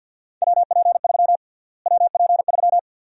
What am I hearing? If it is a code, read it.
Morse code, heard as a keyed tone, about 48 wpm, 700 Hz. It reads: WP3 WP3
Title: USA Call Sign Prefixes - Repeated 2x - 50wpm